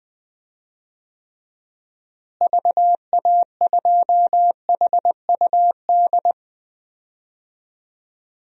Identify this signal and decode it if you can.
Morse code, heard as a keyed tone, about 20 wpm, 700 Hz. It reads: VA2HUD
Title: DXCC-100 Call Signs - 6 Characters - Repeated 2x - 20wpm